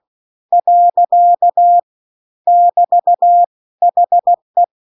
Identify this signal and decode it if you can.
Morse code, heard as a keyed tone, about 16 wpm, 700 Hz. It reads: . <BT> HE
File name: 16WPM